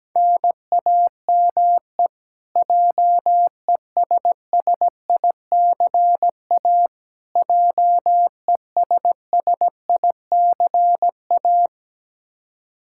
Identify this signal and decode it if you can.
Morse code, heard as a keyed tone, about 17 wpm, 700 Hz. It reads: NAME JESSICA JESSICA
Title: Practice QSO - Exchange 1 - Beginning - 17wpm